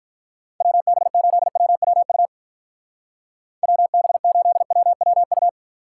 Morse code, heard as a keyed tone, about 42 wpm, 700 Hz. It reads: WB8WWU WB8WWU
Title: Speed Racing - USA 2x3 Call Signs 35wpm